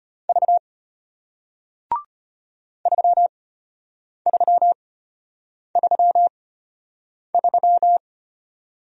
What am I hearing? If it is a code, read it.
Morse code, heard as a keyed tone, about 33 wpm, spaced as for 8 wpm, 700 Hz. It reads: V 3333